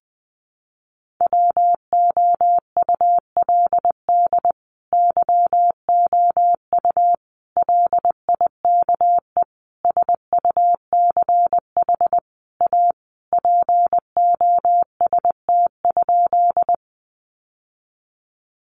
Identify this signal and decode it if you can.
Morse code, heard as a keyed tone, about 20 wpm, 700 Hz. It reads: WOULD YOU LIKE SUCH A POST?
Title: Sentences from Top 1000 Words 20wpm